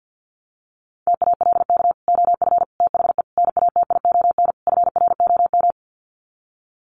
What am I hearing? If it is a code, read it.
Morse code, heard as a keyed tone, about 50 wpm, 700 Hz. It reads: TALK OF THE NATION FROM
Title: Sets of 5 Words 50wpm